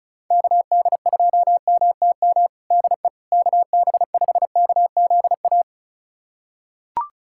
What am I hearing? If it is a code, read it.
Morse code, heard as a keyed tone, about 34 wpm, 700 Hz. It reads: KD2MTM DE KB5KZA
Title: USA Callsign DE Callsign - 6 Characters - 35wpm